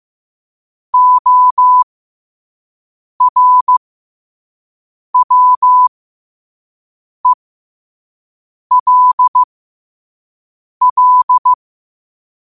Morse code, heard as a keyed tone, about 15 wpm, spaced as for 5 wpm, 1000 Hz. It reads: ORWELL
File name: qotdfiles / 5wpm